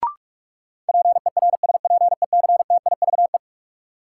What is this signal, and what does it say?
Morse code, heard as a keyed tone, about 46 wpm, 700 Hz. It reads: PERSPECTIVE
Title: Sentences from Top 1200 Words - New Word Drill Alt - 45wpm